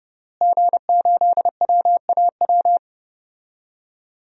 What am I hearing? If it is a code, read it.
Morse code, heard as a keyed tone, about 30 wpm, 700 Hz. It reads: G8WAW
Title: DXCC-100 Call Signs - 5 Characters - Repeated 3x - 30wpm